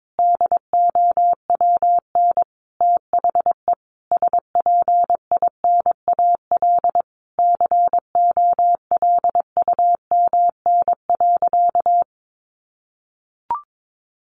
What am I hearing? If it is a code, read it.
Morse code, heard as a keyed tone, about 22 wpm, 700 Hz. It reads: DOWN THE SPINAL COLUMN.
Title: A Sound Of Thunder 22wpm